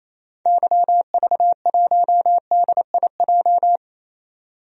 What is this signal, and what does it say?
Morse code, heard as a keyed tone, about 28 wpm, 700 Hz. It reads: YV1DIJ
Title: DXCC-100 Call Signs - 6 Characters - 28wpm